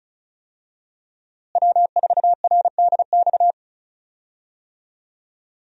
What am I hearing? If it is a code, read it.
Morse code, heard as a keyed tone, about 35 wpm, 700 Hz. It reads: W4RDX
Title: USA Call Signs - 5 Characters - Repeated 2x - 35wpm